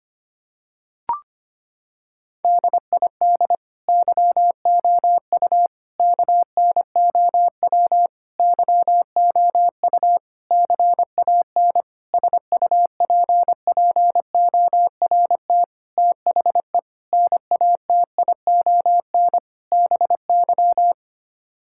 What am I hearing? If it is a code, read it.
Morse code, heard as a keyed tone, about 25 wpm, 700 Hz. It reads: DID YOU KNOW YOU CAN SUPPORT THE NATION BY
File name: Sets of 9 Words - 25wpm